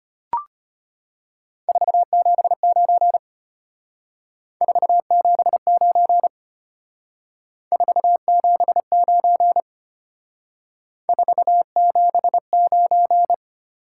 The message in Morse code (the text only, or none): 479 479 479 479